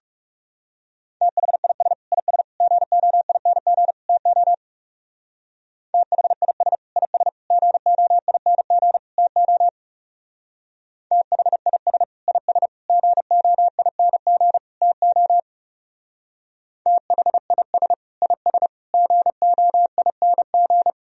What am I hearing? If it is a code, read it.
Morse code, heard as a keyed tone, about 38 wpm, 700 Hz. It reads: THIS IS GOING TO THIS IS GOING TO THIS IS GOING TO THIS IS GOING